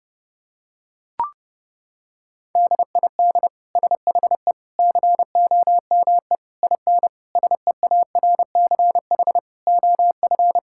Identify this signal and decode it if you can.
Morse code, heard as a keyed tone, about 32 wpm, 700 Hz. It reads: DID SHE COME IN SEARCH OF
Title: Sentences from Top 1000 Words - Review Part 5 - 30wpm